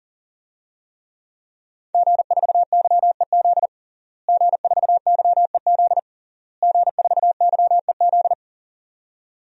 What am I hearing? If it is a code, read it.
Morse code, heard as a keyed tone, about 40 wpm, 700 Hz. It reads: G4YEZ G4YEZ G4YEZ
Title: DXCC-100 Call Signs - 5 Characters - Repeated 3x - 40wpm